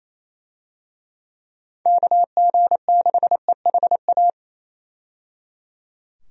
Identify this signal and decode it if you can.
Morse code, heard as a keyed tone, about 27 wpm, 700 Hz. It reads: KG6EHA
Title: USA Call Signs - 6 Characters - 28wpm